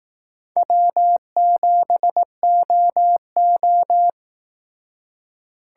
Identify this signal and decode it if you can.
Morse code, heard as a keyed tone, about 18 wpm, 700 Hz. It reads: W7OO
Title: Speed Racing - USA 1x2 Call Signs 15wpm